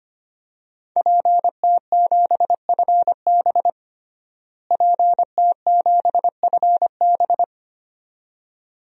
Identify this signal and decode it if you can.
Morse code, heard as a keyed tone, about 25 wpm, 700 Hz. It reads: PT7FB PT7FB
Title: DXCC-100 Call Signs - 5 Characters - Repeated 2x - 25wpm